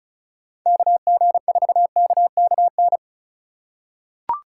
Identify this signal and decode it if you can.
Morse code, heard as a keyed tone, about 36 wpm, 700 Hz. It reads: KG4KKN
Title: USA Call Signs - 6 Characters - 4x Character Spacing - 35wpm